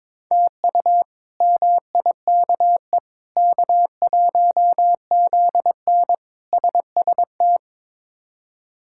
Morse code, heard as a keyed tone, about 22 wpm, 700 Hz. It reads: TU MIKE K1ZN SST
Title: K1USN SST QSO Practice - 22wpm